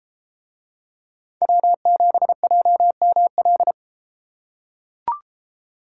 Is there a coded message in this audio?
Yes: Morse code from 1.4 to 3.7 s, heard as a keyed tone, about 33 wpm, 700 Hz. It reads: W7JML